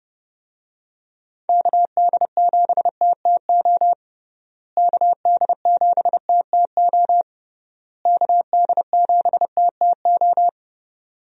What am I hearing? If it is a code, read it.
Morse code, heard as a keyed tone, about 30 wpm, 700 Hz. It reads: KD7TTO KD7TTO KD7TTO
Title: USA Call Signs - 6 Characters - Repeated 3x - 30wpm